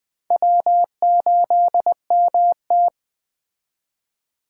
Morse code, heard as a keyed tone, about 20 wpm, 700 Hz. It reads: W8MT